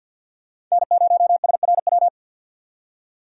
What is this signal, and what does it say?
Morse code, heard as a keyed tone, about 52 wpm, 700 Hz. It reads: N0SRW
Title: USA Call Signs - 3x Character Spacing - 50wpm